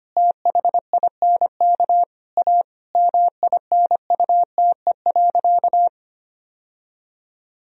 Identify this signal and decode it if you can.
Morse code, heard as a keyed tone, about 25 wpm, 700 Hz. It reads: THINK A MINUTE.
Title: Sentences from Top 1000 Words - Review Part 1 - 25wpm